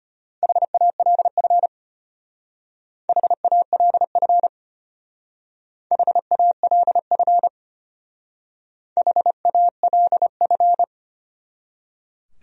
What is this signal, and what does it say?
Morse code, heard as a keyed tone, about 33 wpm, 700 Hz. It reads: HALF HALF HALF HALF